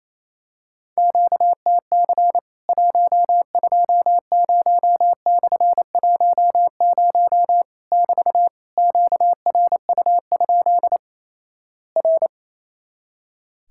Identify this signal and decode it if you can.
Morse code, heard as a keyed tone, about 28 wpm, 700 Hz. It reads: QTC 120/10 <BT> QRU? R
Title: Worked All Europe DX Contest - QTC Exchange - 28wpm